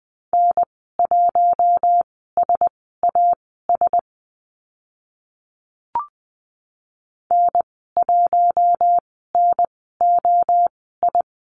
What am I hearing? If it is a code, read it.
Morse code, heard as a keyed tone, about 20 wpm, spaced as for 15 wpm, 700 Hz. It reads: N1SAS N1NOI
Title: 042 Lesson 8g - Number 1 - Call Signs 20wpm